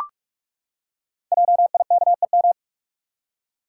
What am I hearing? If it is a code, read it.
Morse code, heard as a keyed tone, about 44 wpm, 690 Hz. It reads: JIKEM
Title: Code Groups - 5 Letters - 45wpm